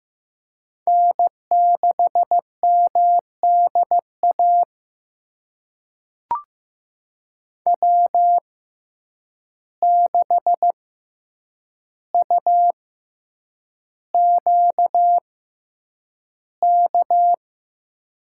Morse code, heard as a keyed tone, about 15 wpm, spaced as for 8 wpm, 700 Hz. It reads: N6MDA W6UQK
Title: USA Call Signs - 5 Characters - 6x Character Spacing - 15wpm